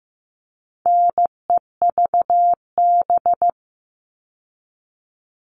Morse code, heard as a keyed tone, about 15 wpm, 700 Hz. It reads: NEVB
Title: All Cut Numbers - 4-Digit 15wpm